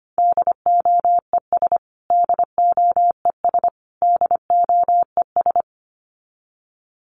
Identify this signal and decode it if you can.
Morse code, heard as a keyed tone, about 25 wpm, 700 Hz. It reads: DOES DOES DOES
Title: Top 300 Words Repeated 25wpm